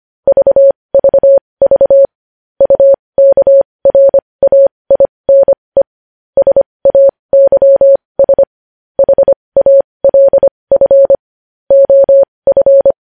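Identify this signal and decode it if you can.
Morse code, heard as a keyed tone, about 25 wpm, spaced as for 20 wpm, 550 Hz. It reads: VVV UKRAINE SAYS HALF OF
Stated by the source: Morse code news headlines practice at 20.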